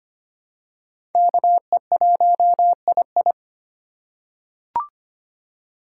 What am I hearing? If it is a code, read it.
Morse code, heard as a keyed tone, about 25 wpm, 700 Hz. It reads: KE1II
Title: USA Call Signs - 5 Characters - 4x Character Spacing - 25wpm